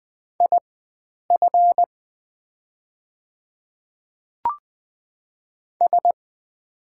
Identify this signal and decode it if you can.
Morse code, heard as a keyed tone, about 20 wpm, spaced as for 9 wpm, 700 Hz. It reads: IF S
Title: 4x Character Spacing - Top 100 Words - 20wpm